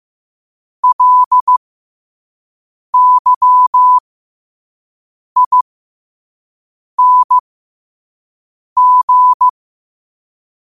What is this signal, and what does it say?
Morse code, heard as a keyed tone, about 15 wpm, spaced as for 5 wpm, 1000 Hz. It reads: LYING
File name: 5wpm